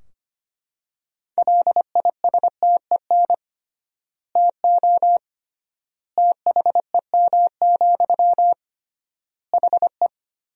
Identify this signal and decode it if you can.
Morse code, heard as a keyed tone, about 25 wpm, 700 Hz. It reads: LISTEN TO THEM, HE
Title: 3x Word-Spacing - Sentences from Top 400 Words - 25wpm